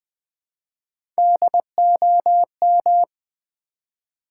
Morse code, heard as a keyed tone, about 20 wpm, 700 Hz. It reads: DOM
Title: Build A Word - 3 Letters At A Time - 8-Letter Words - 20wpm